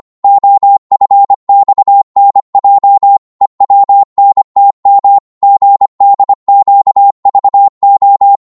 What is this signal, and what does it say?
Morse code, heard as a keyed tone, about 25 wpm, 800 Hz. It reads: OFXNJ EWNTM GDQVO
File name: Morse / ltr-25wpm